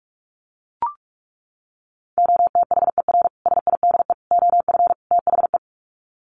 Morse code, heard as a keyed tone, about 44 wpm, 700 Hz. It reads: OTHER SIDE OF THE
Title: Sets of 4 Words 45wpm